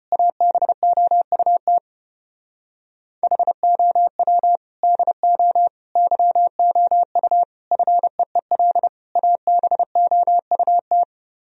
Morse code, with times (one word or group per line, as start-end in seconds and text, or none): 0.1-1.8 ABOUT
3.2-4.6 HOW
4.8-5.7 DO
6.0-7.4 YOU
7.7-8.9 FEEL
9.2-11.0 ABOUT